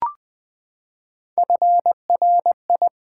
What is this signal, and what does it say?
Morse code, heard as a keyed tone, about 20 wpm, 690 Hz. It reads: FRI
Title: Build A Word - 3 Letters At A Time - 8-Letter Words - 20wpm